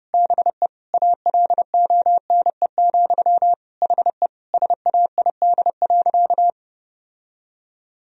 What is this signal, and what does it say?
Morse code, heard as a keyed tone, about 30 wpm, 700 Hz. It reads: BE ALONE, HE SAID.
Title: Sentences from Top 1000 Words - Review Part 3 - 30wpm